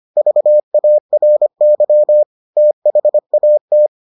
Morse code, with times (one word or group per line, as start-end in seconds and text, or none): 0.2-2.2 VARY
2.6-3.9 THAT